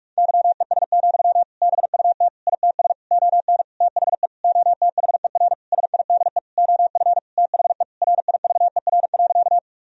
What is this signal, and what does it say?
Morse code, heard as a keyed tone, about 41 wpm, 700 Hz. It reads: YES, BUT ITS ON THE OTHER SIDE OF THE RIVER.
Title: Sentences from Top 1000 Words - Review Part 1 - 45wpm